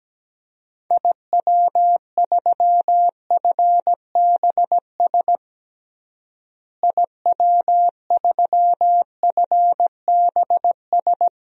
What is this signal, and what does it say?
Morse code, heard as a keyed tone, about 17 wpm, 700 Hz. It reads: IW3FBS IW3FBS
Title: DXCC-100 Call Signs - 6 Characters - Repeated 2x - 17wpm